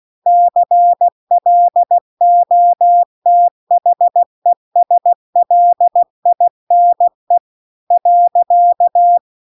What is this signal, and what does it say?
Morse code, heard as a keyed tone, about 16 wpm, 700 Hz. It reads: CLOTHESLINE .